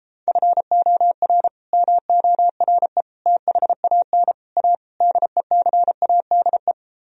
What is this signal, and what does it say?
Morse code, heard as a keyed tone, about 37 wpm, 700 Hz. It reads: FOR MORE THAN A DECADE